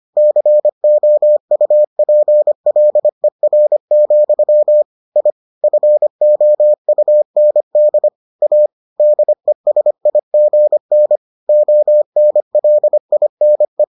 Morse code, heard as a keyed tone, about 25 wpm, 600 Hz. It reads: COUPLER, I FOUND A DESIGN ONLINE